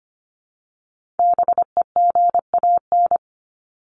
Morse code, heard as a keyed tone, about 25 wpm, 700 Hz. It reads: BEGAN
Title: Top 300 Words 25wpm